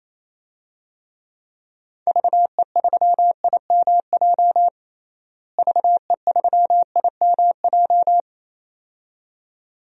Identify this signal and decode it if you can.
Morse code, heard as a keyed tone, about 28 wpm, 700 Hz. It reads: VE3IMJ VE3IMJ
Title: DXCC-100 Call Signs - 6 Characters - Repeated 2x - 28wpm